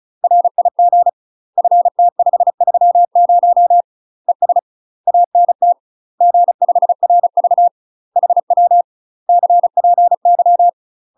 Morse code, heard as a keyed tone, about 36 wpm, 700 Hz. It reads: RIG FT530 ES ANT G5RV HW CPY
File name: UK MASTER QSOs TRAINING / 35WPM / UK MASTER QSOs 0031